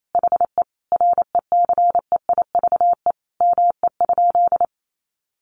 Morse code, heard as a keyed tone, about 28 wpm, 700 Hz. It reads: HE RECEIVE ME?
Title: Sentences from Top 1000 Words 28wpm